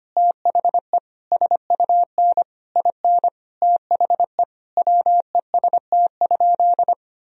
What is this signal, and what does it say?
Morse code, heard as a keyed tone, about 25 wpm, 700 Hz. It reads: THE SUN IN THE WEST?
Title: Sentences from Top 1000 Words - Review Part 2 - 25wpm